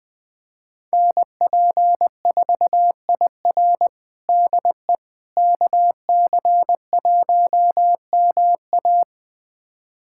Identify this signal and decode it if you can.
Morse code, heard as a keyed tone, about 20 wpm, 700 Hz. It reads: NP4IR DE KC1MA
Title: USA Callsign DE Callsign - 5 Characters - 20wpm